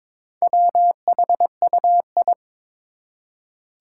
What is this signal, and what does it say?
Morse code, heard as a keyed tone, about 22 wpm, 700 Hz. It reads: WHUI
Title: Four Challenging Letter-Number Combinations 22wpm